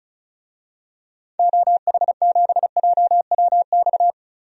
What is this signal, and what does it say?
Morse code, heard as a keyed tone, about 36 wpm, 700 Hz. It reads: OH7JWX
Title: DXCC-100 Call Signs - 6 Characters - 35wpm